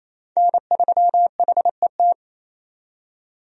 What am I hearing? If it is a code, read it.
Morse code, heard as a keyed tone, about 27 wpm, 700 Hz. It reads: N3HET
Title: USA Call Signs - 5 Characters - 28wpm